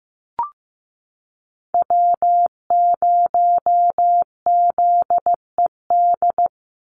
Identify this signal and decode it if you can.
Morse code, heard as a keyed tone, about 15 wpm, 700 Hz. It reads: W0ZED
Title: USA Call Signs 15wpm